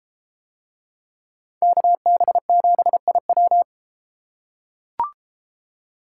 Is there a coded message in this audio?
Yes: Morse code from 1.6 to 3.6 s, heard as a keyed tone, about 31 wpm, 700 Hz. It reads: KB7IW